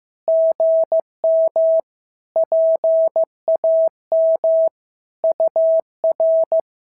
Morse code, heard as a keyed tone, about 15 wpm, 650 Hz. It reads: GM PAM UR
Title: SOTA QSO Practice 15wpm